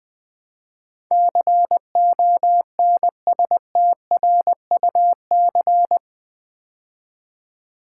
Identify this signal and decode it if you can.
Morse code, heard as a keyed tone, about 20 wpm, 700 Hz. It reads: CONSTRUC
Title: Build A Word - 4 Letters At A Time - 12-Letter Words - 20wpm